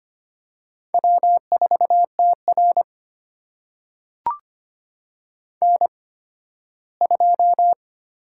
Morse code, heard as a keyed tone, about 25 wpm, spaced as for 15 wpm, 700 Hz. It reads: W4TR N2
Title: USA Call Signs - 4 Characters - 8x Character Spacing - 25wpm